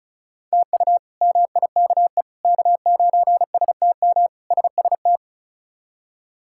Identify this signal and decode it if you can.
Morse code, heard as a keyed tone, about 34 wpm, 700 Hz. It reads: TU MIKE K9STM SST
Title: K1USN SST QSO Practice - 35wpm